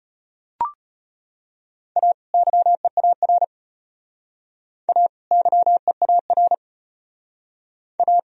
Morse code, heard as a keyed tone, about 32 wpm, 700 Hz. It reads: A YEAR A YEAR A